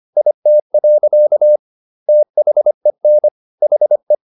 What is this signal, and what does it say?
Morse code, heard as a keyed tone, about 25 wpm, 600 Hz. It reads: IT. THEN HE